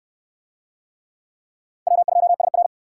Morse code, heard as a keyed tone, about 63 wpm, 700 Hz. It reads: W2SR